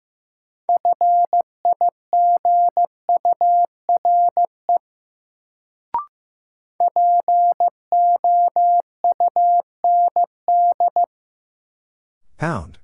Morse code, heard as a keyed tone, about 15 wpm, 700 Hz. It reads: FIGURE POUND
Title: Sentences from Top 1000 Words - Review Part 2 - New Word Drill - 15wpm